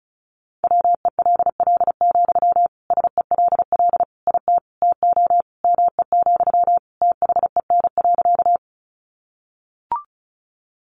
Morse code, heard as a keyed tone, about 34 wpm, 700 Hz. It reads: WELL, SELL IT TO ME, THEN.
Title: Sentences from Top 1100 Words 35wpm